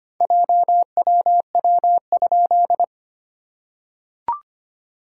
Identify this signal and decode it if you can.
Morse code, heard as a keyed tone, about 25 wpm, 700 Hz. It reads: JWW?
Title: SOTA QSO Practice 25wpm